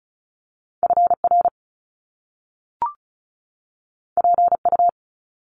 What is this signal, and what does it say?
Morse code, heard as a keyed tone, about 36 wpm, 700 Hz. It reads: FR PU
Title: Top Two Letter Combinations in Words 35wpm